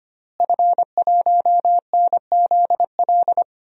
Morse code, heard as a keyed tone, about 25 wpm, 700 Hz. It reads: F1NZL
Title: DXCC-100 Call Signs - 5 Characters - Repeated 3x - 25wpm